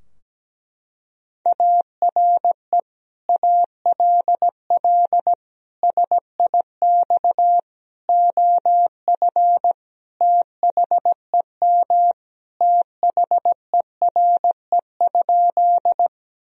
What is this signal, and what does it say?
Morse code, heard as a keyed tone, about 17 wpm, 700 Hz. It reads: ARE ALL SIX OF THEM THERE?
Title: Sentences from Top 1000 Words - Review Part 2 - 17wpm